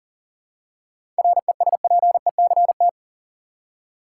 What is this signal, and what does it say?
Morse code, heard as a keyed tone, about 40 wpm, 700 Hz. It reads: RESPECT
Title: Seven Letter Words - 40wpm